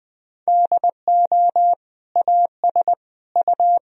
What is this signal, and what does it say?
Morse code, heard as a keyed tone, about 20 wpm, 700 Hz. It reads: DO AS U
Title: Sentences from Top 100 Words and Abbreviations 20wpm